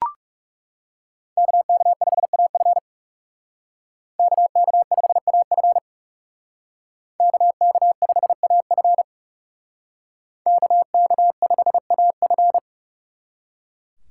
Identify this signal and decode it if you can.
Morse code, heard as a keyed tone, about 36 wpm, 700 Hz. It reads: KK5AF KK5AF KK5AF KK5AF
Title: Speed Racing - USA 2x2 Call Signs 30wpm